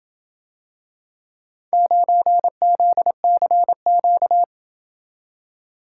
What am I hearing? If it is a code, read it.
Morse code, heard as a keyed tone, about 27 wpm, 700 Hz. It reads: 9ZCQ